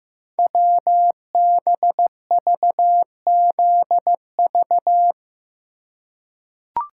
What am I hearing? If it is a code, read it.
Morse code, heard as a keyed tone, about 15 wpm, 700 Hz. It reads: WBVZV
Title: Code Groups - 5 Letters - 15wpm